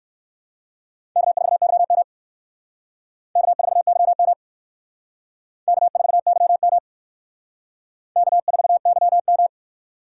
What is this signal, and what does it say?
Morse code, heard as a keyed tone, about 55 wpm, 700 Hz. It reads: K4YM K4YM K4YM K4YM
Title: Speed Racing - USA 1x2 Call Signs 45wpm